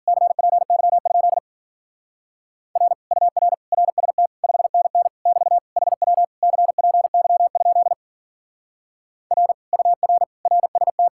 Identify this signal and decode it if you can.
Morse code, heard as a keyed tone, about 47 wpm, 700 Hz. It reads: CPY? R UR RST 5NN <BT> HW CPY? R UR RST
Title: Speed-Racing Common QSO Phrases - 40wpm